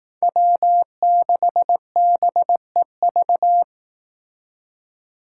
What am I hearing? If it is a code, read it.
Morse code, heard as a keyed tone, about 19 wpm, 700 Hz. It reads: W6BEV
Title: Speed Racing - USA 1x3 Call Signs 15wpm